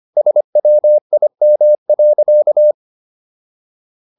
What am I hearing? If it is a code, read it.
Morse code, heard as a keyed tone, about 25 wpm, 600 Hz. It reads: SWIM.